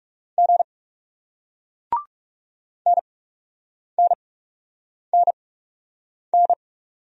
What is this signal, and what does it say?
Morse code, heard as a keyed tone, about 41 wpm, spaced as for 8 wpm, 700 Hz. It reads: G NNNN